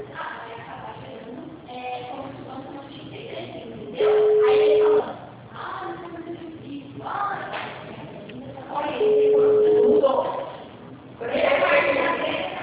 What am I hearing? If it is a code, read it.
Ringback tone (Europe, 425 Hz), with speech.